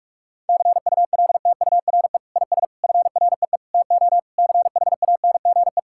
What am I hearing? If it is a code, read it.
Morse code, heard as a keyed tone, about 46 wpm, 700 Hz. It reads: CULTURE IS FREE TO CHANGE
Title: Sentences from Top 1000 Words - Review Part 6 - 45wpm